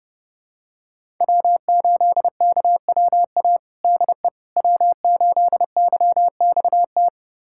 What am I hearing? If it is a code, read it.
Morse code, heard as a keyed tone, about 30 wpm, 700 Hz. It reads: W8KWA DE W8YXT
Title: USA Callsign DE Callsign - 5 Characters - 30wpm